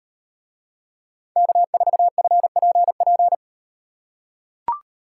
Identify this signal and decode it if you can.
Morse code, heard as a keyed tone, about 38 wpm, 700 Hz. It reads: K4FPP